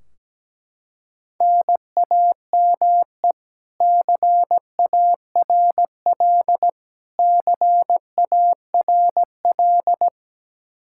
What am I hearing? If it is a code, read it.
Morse code, heard as a keyed tone, about 17 wpm, 700 Hz. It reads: NAME CARL CARL
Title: K1USN SST QSO Practice 17wpm